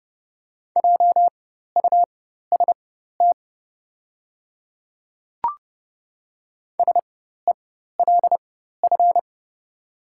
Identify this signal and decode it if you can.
Morse code, heard as a keyed tone, about 30 wpm, spaced as for 14 wpm, 700 Hz. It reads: JUST SELF
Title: 4x Character Spacing - Top 200 Words - 30wpm